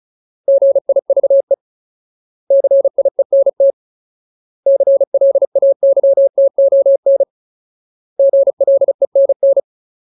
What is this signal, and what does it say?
Morse code, heard as a keyed tone, about 34 wpm, 550 Hz. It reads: GIVE CIENT CLAYTON GLENN
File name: Ultimate Word List / 35WPM Ultimate